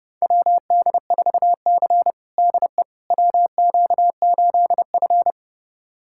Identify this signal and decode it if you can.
Morse code, heard as a keyed tone, about 30 wpm, 700 Hz. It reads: WD4C DE WQ8F
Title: USA Callsign DE Callsign - 4 Characters - 30wpm